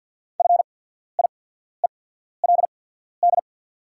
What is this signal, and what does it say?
Morse code, heard as a keyed tone, about 48 wpm, spaced as for 14 wpm, 690 Hz. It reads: FIELD
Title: 8x Character Spacing - Top 500 Words - 50wpm